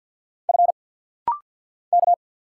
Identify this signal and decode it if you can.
Morse code, heard as a keyed tone, about 49 wpm, 690 Hz. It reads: F K
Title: Single Letters - Rapid-Fire 50wpm